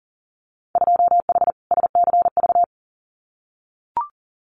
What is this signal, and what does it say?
Morse code, heard as a keyed tone, about 40 wpm, 700 Hz. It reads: 2H SCV
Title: Winter Field Day Exchanges Encore 40wpm